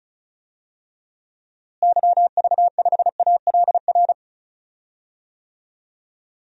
Morse code, heard as a keyed tone, about 34 wpm, 700 Hz. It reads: YV5ALR
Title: DXCC-100 Call Signs - 6 Characters - Repeated 2x - 35wpm